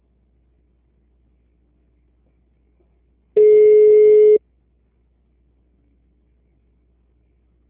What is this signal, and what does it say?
Ringback tone (Europe, 425 Hz).
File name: out- -2007- - - .14